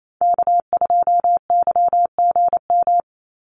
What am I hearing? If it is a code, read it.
Morse code, heard as a keyed tone, about 28 wpm, 700 Hz. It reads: K2YGM
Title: Call Signs - 28wpm